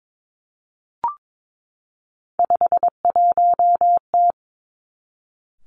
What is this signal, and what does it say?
Morse code, heard as a keyed tone, about 22 wpm, 700 Hz. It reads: 51T
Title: Three Challenging Letter-Number Combinations 22wpm